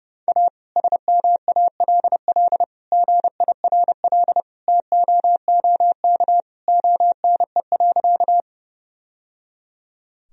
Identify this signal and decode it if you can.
Morse code, heard as a keyed tone, about 30 wpm, 700 Hz. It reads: A SMALL GIRL TOOK ONE.
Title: Sentences from Top 1000 Words - Review Part 1 - 30wpm